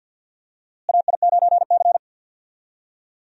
Morse code, heard as a keyed tone, about 52 wpm, 700 Hz. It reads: AI9C
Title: USA Call Signs - 4 Characters - Repeated 3x - 50wpm